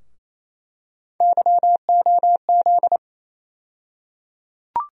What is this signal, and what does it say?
Morse code, heard as a keyed tone, about 28 wpm, 700 Hz. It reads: YOZ
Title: USA Call Sign Postfixes - 3 Letters 28wpm